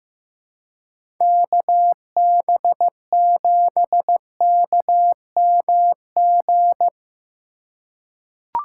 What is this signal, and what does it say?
Morse code, heard as a keyed tone, about 16 wpm, 700 Hz. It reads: KB7KMG
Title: USA Call Signs - 6 Characters - 3x Character Spacing - 15wpm